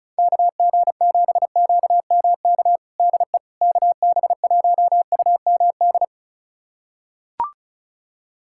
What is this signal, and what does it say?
Morse code, heard as a keyed tone, about 36 wpm, 700 Hz. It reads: KG7QMK DE KB1UMD
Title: USA Callsign DE Callsign - 6 Characters - 35wpm